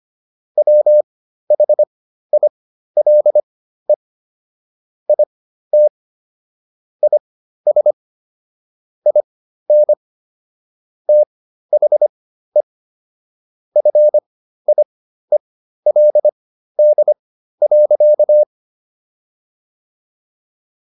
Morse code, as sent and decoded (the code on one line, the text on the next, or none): .-- .... .. .-.. / .. - / .. ... / .. -. / - .... / ..-. .. .-.. -.. .-.-.-
WHILE IT IS IN THE FIELD.